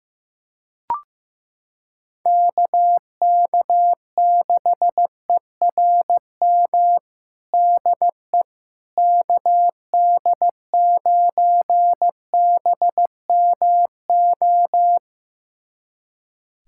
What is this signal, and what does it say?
Morse code, heard as a keyed tone, about 15 wpm, 700 Hz. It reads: KK6ERM DE KD9BMO
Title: USA Callsign DE Callsign - 6 Characters - 15wpm